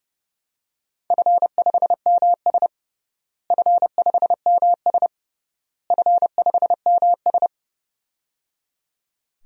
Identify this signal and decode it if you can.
Morse code, heard as a keyed tone, about 30 wpm, 700 Hz. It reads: F5MS F5MS F5MS
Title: DXCC-100 Call Signs - 4 Characters - Repeated 3x - 30wpm